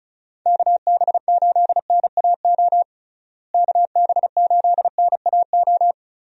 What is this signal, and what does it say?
Morse code, heard as a keyed tone, about 34 wpm, 700 Hz. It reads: KB8NAO KB8NAO
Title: USA Call Signs - 6 Characters - Repeated 2x - 35wpm